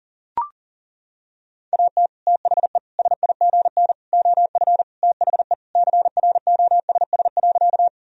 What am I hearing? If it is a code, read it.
Morse code, heard as a keyed tone, about 40 wpm, 700 Hz. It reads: AT THE SIGN OF THE CROSS.
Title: Sentences from Top 1000 Words - Review Part 1 - 40wpm